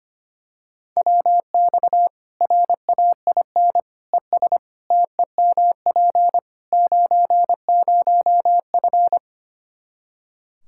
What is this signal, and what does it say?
Morse code, heard as a keyed tone, about 25 wpm, 700 Hz. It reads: WX RAIN ES TEMP 90F
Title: Practice QSO - Exchange 2 - Info - 25wpm